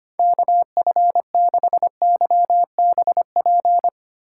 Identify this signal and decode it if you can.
Morse code, heard as a keyed tone, about 25 wpm, 700 Hz. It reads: KF6YBP
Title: USA Call Signs - 6 Characters - 6x Character Spacing - 25wpm